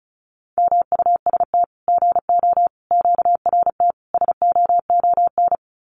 Morse code, heard as a keyed tone, about 34 wpm, 700 Hz. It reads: MUST GO QRT SOON
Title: Common QSO Phrases 35wpm